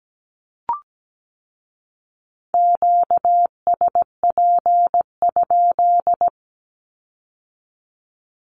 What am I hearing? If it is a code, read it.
Morse code, heard as a keyed tone, about 17 wpm, 700 Hz. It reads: QSP?
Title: Q-Codes with Spoken Meaning - 17wpm